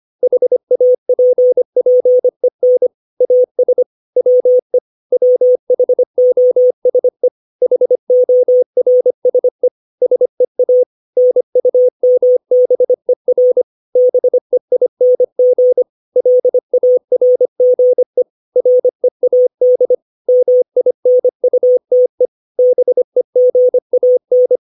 Morse code, heard as a keyed tone, about 25 wpm, 500 Hz. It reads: HAPPEN AS WE WHOSE HORSE SEA NUMBER BEING LARGE READ MINUTE BEGAN
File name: 632 words / 632 Words 25WPM